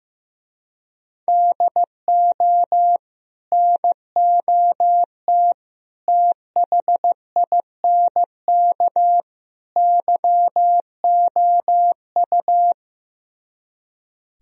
Morse code, heard as a keyed tone, about 15 wpm, 700 Hz. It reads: DO NOT THINK YOU
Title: Speed-Racing - Sets of 4 Words 15wpm